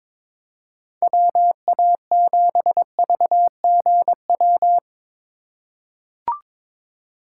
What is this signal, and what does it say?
Morse code, heard as a keyed tone, about 22 wpm, 700 Hz. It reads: WA7VGW
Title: USA Call Signs - 6 Characters - 3x Character Spacing - 22wpm